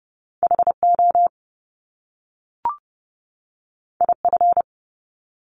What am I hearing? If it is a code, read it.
Morse code, heard as a keyed tone, about 30 wpm, 700 Hz. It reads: HO IF
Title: Top Two Letter Combinations in Words 30wpm